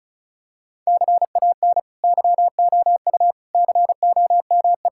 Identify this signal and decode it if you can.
Morse code, heard as a keyed tone, about 36 wpm, 700 Hz. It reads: CAN YOU COME
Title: Sentences from Top 1000 Words - Review Part 5 - 35wpm